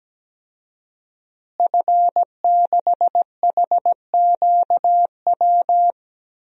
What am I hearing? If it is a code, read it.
Morse code, heard as a keyed tone, about 17 wpm, 700 Hz. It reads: F6HQW
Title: DXCC-100 Call Signs - 5 Characters - Repeated 2x - 17wpm